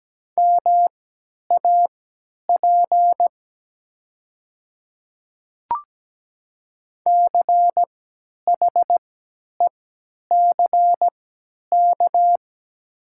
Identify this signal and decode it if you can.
Morse code, heard as a keyed tone, about 17 wpm, spaced as for 10 wpm, 700 Hz. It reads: MAP CHECK
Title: 3x Character Spacing - Top 500 Words - 17wpm